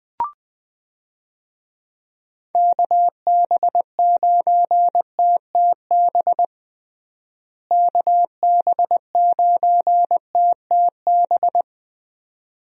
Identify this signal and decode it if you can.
Morse code, heard as a keyed tone, about 21 wpm, 700 Hz. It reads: KB9TTB KB9TTB
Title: USA Call Signs - 6 Characters - Repeated 2x - 20wpm